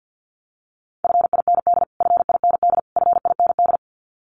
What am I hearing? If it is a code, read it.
Morse code, heard as a keyed tone, about 48 wpm, 700 Hz. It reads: FIND FIND FIND
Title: Top 200 Words Repeated 50wpm